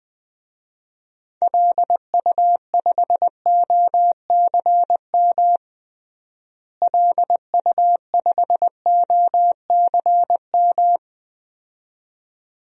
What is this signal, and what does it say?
Morse code, heard as a keyed tone, about 21 wpm, 700 Hz. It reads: LU5OCM LU5OCM
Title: DXCC-100 Call Signs - 6 Characters - Repeated 2x - 20wpm